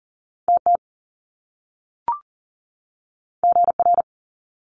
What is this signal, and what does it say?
Morse code, heard as a keyed tone, about 40 wpm, 700 Hz. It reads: TT GR
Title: Top Two Letter Combinations in Words 40wpm